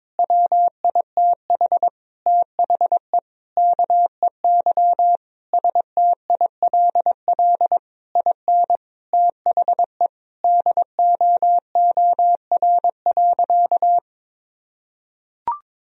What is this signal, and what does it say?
Morse code, heard as a keyed tone, about 21 wpm, 700 Hz. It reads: WITH THE KEY STILL IN THE DOOR.
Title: Sentences from Top 1000 Words - Review Part 4 - 22wpm